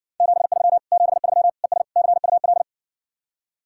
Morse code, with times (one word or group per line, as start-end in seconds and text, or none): 0.2-0.8 73
0.9-1.5 73
1.6-1.8 ES
2.0-2.6 CUL